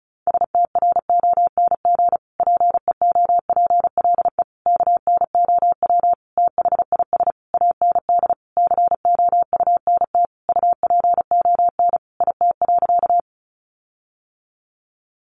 Morse code, heard as a keyed tone, about 34 wpm, 700 Hz. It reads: STRONG PEOPLE KNOW THIS AND COUNT UPON IT.
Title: Sentences from Top 1500 Words 35wpm